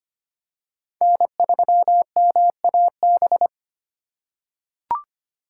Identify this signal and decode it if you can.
Morse code, heard as a keyed tone, about 25 wpm, 700 Hz. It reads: N3MAB
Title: USA Call Signs - 5 Characters - 2x Character Spacing - 25wpm